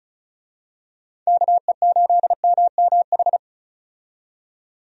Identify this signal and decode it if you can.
Morse code, heard as a keyed tone, about 34 wpm, 700 Hz. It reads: KE8MMH
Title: USA Call Signs - 6 Characters - 35wpm